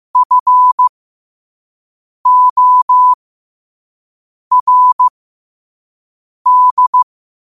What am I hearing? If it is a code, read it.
Morse code, heard as a keyed tone, about 15 wpm, spaced as for 5 wpm, 1000 Hz. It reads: FORD
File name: qotdfiles / 5wpm